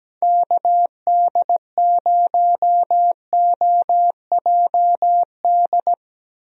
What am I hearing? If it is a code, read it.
Morse code, heard as a keyed tone, about 17 wpm, 700 Hz. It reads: KD0OJD
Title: USA Call Signs - 6 Characters - Repeated 2x - 17wpm